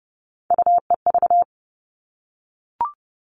Morse code, heard as a keyed tone, about 30 wpm, 700 Hz. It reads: UEV
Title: All Cut Numbers - 3-Digit 30wpm